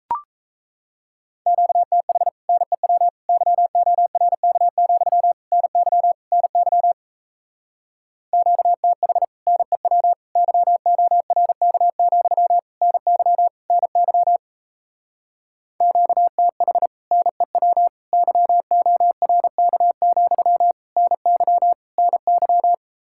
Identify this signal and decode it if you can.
Morse code, heard as a keyed tone, about 37 wpm, 700 Hz. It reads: QTH NEW YORK, NY NY QTH NEW YORK, NY NY QTH NEW YORK, NY NY